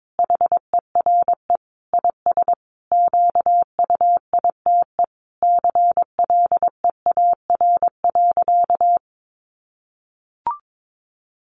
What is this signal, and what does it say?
Morse code, heard as a keyed tone, about 22 wpm, 700 Hz. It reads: HERE IS QUITE CLEAR.
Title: Sentences from Top 1100 Words 22wpm